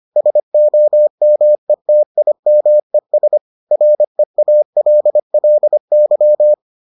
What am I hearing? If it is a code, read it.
Morse code, heard as a keyed tone, about 25 wpm, 600 Hz. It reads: SOMETIMES REALLY